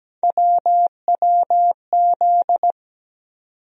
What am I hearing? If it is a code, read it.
Morse code, heard as a keyed tone, about 17 wpm, 700 Hz. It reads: WWZ
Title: USA Call Sign Postfixes - 3 Letters 17wpm